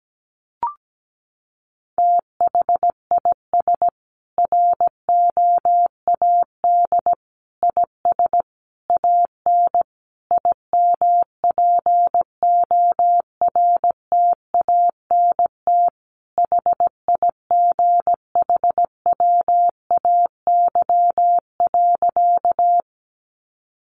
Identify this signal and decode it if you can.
Morse code, heard as a keyed tone, about 17 wpm, 700 Hz. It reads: THIS ROAD IS AN IMPORTANT HIGHWAY.
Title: Sentences from Top 1400 Words 17wpm